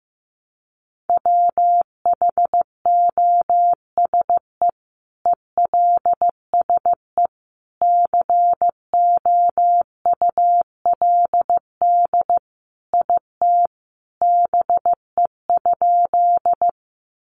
Morse code, heard as a keyed tone, about 15 wpm, 700 Hz. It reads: WHOSE ELSE COULD IT BE?
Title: Sentences from Top 1200 Words 15wpm